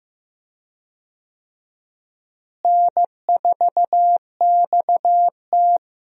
Morse code, heard as a keyed tone, about 15 wpm, 700 Hz. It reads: N4XT
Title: USA Call Signs - 4 Characters - Repeated 2x - 15wpm